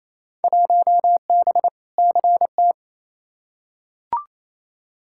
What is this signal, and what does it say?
Morse code, heard as a keyed tone, about 28 wpm, 700 Hz. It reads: 1B CT
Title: Field Day Exchanges Encore - 28wpm